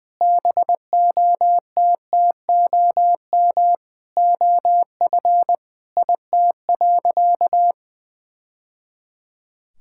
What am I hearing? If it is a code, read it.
Morse code, heard as a keyed tone, about 20 wpm, 700 Hz. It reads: BOTTOM OF IT.
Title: Sentences from Top 1000 Words - Review Part 6 - 20wpm